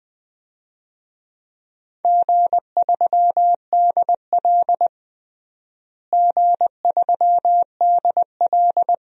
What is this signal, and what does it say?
Morse code, heard as a keyed tone, about 20 wpm, 700 Hz. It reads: G3DL G3DL
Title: DXCC-100 Call Signs - 4 Characters - Repeated 3x - 20wpm